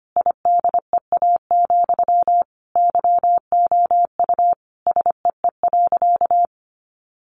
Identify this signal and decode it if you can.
Morse code, heard as a keyed tone, about 25 wpm, 700 Hz. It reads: IDEA, YOU SEE.
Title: Sentences from Top 600 Words 25wpm